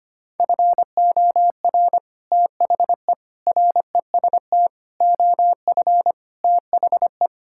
Morse code, heard as a keyed tone, about 25 wpm, 700 Hz. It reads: FOR THE REST OF THE
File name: practice / 25wpm / Sentences with 3-Word Phrases - 25wpm